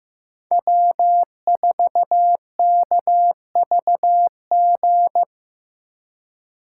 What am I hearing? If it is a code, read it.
Morse code, heard as a keyed tone, about 15 wpm, 700 Hz. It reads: W4KVG
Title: USA Call Signs - 4x Character Spacing - 15wpm